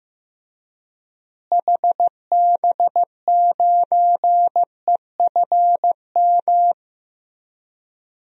Morse code, heard as a keyed tone, about 15 wpm, 700 Hz. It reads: HB9EFM
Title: DXCC-100 Call Signs - 6 Characters - Repeated 2x - 15wpm